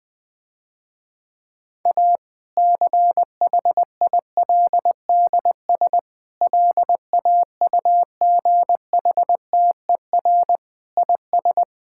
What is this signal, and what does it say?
Morse code, heard as a keyed tone, about 20 wpm, 700 Hz. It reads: A CHILDS LAUGHTER IS
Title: Difficult Sentences with Hints - 20wpm